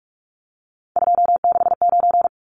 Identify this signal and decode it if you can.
Morse code, heard as a keyed tone, about 44 wpm, 700 Hz. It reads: 269
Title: Numbers - 3-Digit 45wpm